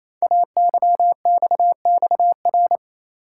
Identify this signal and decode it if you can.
Morse code, heard as a keyed tone, about 27 wpm, 700 Hz. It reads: AYXXR
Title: Code Groups - 5 Letters - 28wpm